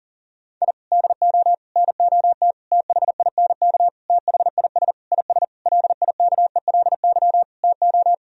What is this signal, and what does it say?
Morse code, heard as a keyed tone, about 40 wpm, 700 Hz. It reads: I DO NOT THINK THIS IS LIKELY TO
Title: Sentences from Top 1000 Words - Review Part 4 - 40wpm